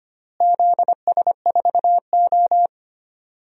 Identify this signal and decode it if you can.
Morse code, heard as a keyed tone, about 25 wpm, 700 Hz. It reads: ZS4O
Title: DXCC-100 Call Signs - 4 Characters - Repeated 3x - 25wpm